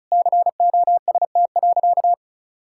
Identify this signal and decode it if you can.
Morse code, heard as a keyed tone, about 36 wpm, 700 Hz. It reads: COST.
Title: Sentences from Top 1000 Words - Review Part 5 - 35wpm